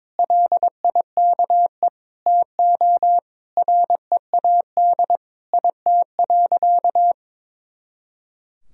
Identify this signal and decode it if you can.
Morse code, heard as a keyed tone, about 22 wpm, 700 Hz. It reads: LIKE TO READ IT.
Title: Sentences from Top 500 Words and Abbreviations 22wpm